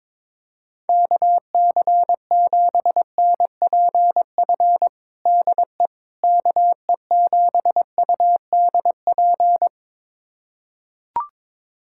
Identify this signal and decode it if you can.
Morse code, heard as a keyed tone, about 22 wpm, 700 Hz. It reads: KC7NPF DE KE7UDP
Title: USA Callsign DE Callsign - 6 Characters - 22wpm